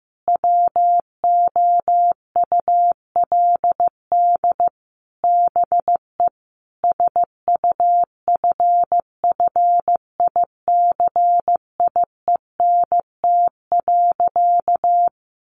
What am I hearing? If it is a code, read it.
Morse code, heard as a keyed tone, about 14 wpm, 700 Hz. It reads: WOULD BE SUFFICIENT.
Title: Sentences from Top 1400 Words 15wpm